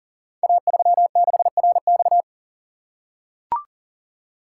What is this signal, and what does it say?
Morse code, heard as a keyed tone, about 41 wpm, 700 Hz. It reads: A36RX
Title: Code Groups - 5 Letter:Numbers - 40wpm